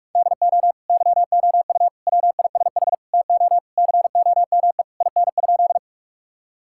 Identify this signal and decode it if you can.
Morse code, heard as a keyed tone, about 46 wpm, 700 Hz. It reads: DO YOU WISH TO COME IN?
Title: Sentences from Top 1600 Words 45wpm